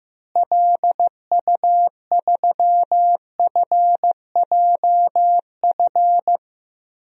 Morse code, heard as a keyed tone, about 16 wpm, 700 Hz. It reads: LU3FJF
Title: DXCC-100 Call Signs - 6 Characters - Repeated 2x - 15wpm